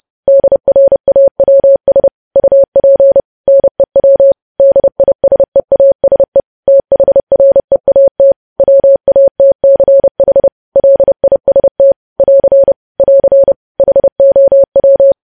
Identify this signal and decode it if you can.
Morse code, heard as a keyed tone, about 30 wpm, 550 Hz. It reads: DRAWS UP NEW DISEASE THREAT WATCH LIST <AR> <AR> HOW
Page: news headlines in morse code at 30 wpm